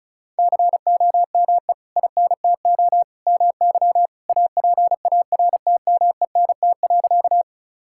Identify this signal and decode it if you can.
Morse code, heard as a keyed tone, about 34 wpm, 700 Hz. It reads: COME INTO MY APARTMENT.
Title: Sentences from Top 1000 Words - Review Part 6 - 35wpm